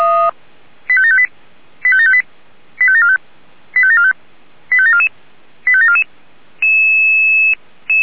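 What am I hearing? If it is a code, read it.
Five-tone selective call (ZVEI), heard as quick runs of five tones, tones of about 70 ms. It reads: 85758 85758 85745 85745 85744 85744
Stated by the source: Five-tone ZVEI selective call received on an SDR.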